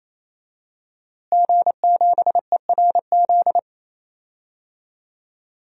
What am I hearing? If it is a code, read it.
Morse code, heard as a keyed tone, about 28 wpm, 700 Hz. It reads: G7ERZ
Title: DXCC-100 Call Signs - 5 Characters - Repeated 3x - 28wpm